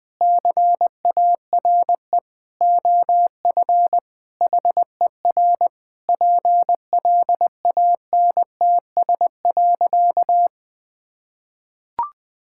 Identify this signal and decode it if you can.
Morse code, heard as a keyed tone, about 20 wpm, 700 Hz. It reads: CARE OF HER PLANTS.
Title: Sentences with 4-Word Phrases - Spoken Beginning - 20wpm